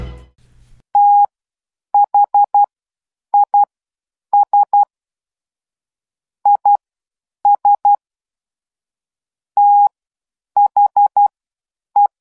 Morse code, heard as a keyed tone, about 12 wpm, spaced as for 8 wpm, 820 Hz. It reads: THIS IS THE